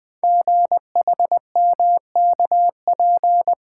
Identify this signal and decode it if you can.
Morse code, heard as a keyed tone, about 20 wpm, 700 Hz. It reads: GHMKP
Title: Code Groups - 5 Letters - 20wpm